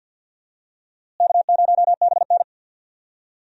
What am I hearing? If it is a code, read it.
Morse code, heard as a keyed tone, about 53 wpm, 700 Hz. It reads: K0BN
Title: USA Call Signs - 4 Characters - Repeated 3x - 50wpm